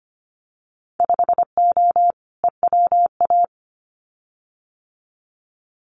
Morse code, heard as a keyed tone, about 25 wpm, 700 Hz. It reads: 5O EWA
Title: Winter Field Day Exchanges Encore 25wpm